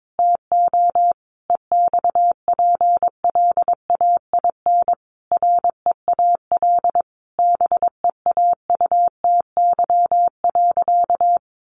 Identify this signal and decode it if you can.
Morse code, heard as a keyed tone, about 22 wpm, 700 Hz. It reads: TO EXPLAIN REAL BEAUTY.
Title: Sentences from Top 1100 Words 22wpm